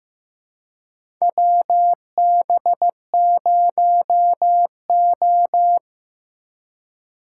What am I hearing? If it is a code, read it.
Morse code, heard as a keyed tone, about 15 wpm, 700 Hz. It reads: WB0O
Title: USA Call Signs - 4 Characters - 2x Character Spacing - 15wpm